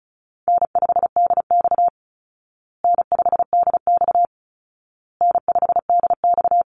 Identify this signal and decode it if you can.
Morse code, heard as a keyed tone, about 34 wpm, 700 Hz. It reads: N5DX N5DX N5DX
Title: Easy Call Signs Repeated 3x - 35wpm